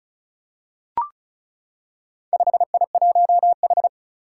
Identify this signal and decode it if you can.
Morse code, heard as a keyed tone, about 35 wpm, 700 Hz. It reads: 5I1H
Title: Four Challenging Letter-Number Combinations 35wpm